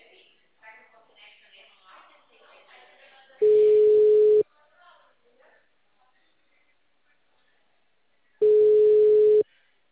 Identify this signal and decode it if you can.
Ringback tone (Europe, 425 Hz), with speech.